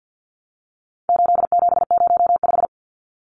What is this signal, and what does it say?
Morse code, heard as a keyed tone, about 50 wpm, 700 Hz. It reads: 8705
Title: Numbers - 4-Digit 50wpm